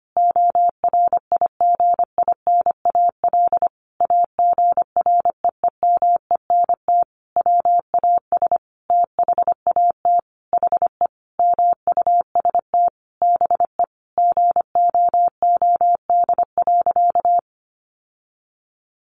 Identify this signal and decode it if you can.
Morse code, heard as a keyed tone, about 25 wpm, 700 Hz. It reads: ORIGINAL AGREEMENT WAS THAT HE MUST BE GOOD.
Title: Sentences from Top 1500 Words 25wpm